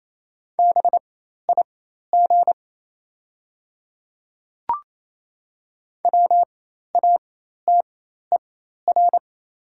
Morse code, heard as a keyed tone, about 28 wpm, spaced as for 13 wpm, 700 Hz. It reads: BIG WATER
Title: 4x Character Spacing - Top 300 Words - 28wpm